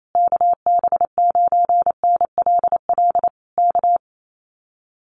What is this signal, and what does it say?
Morse code, heard as a keyed tone, about 28 wpm, 700 Hz. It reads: KB9NLL K
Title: Common QSO Phrases Encore 28wpm